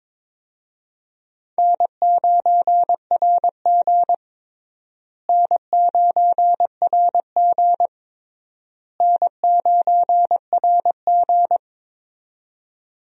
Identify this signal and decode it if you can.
Morse code, heard as a keyed tone, about 22 wpm, 700 Hz. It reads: N9RG N9RG N9RG
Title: USA Call Signs - 4 Characters - Repeated 3x - 22wpm